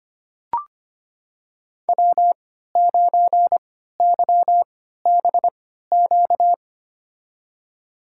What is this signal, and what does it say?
Morse code, heard as a keyed tone, about 25 wpm, spaced as for 14 wpm, 700 Hz. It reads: W9YBQ
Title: USA Call Signs - 5 Characters - 3x Character Spacing - 25wpm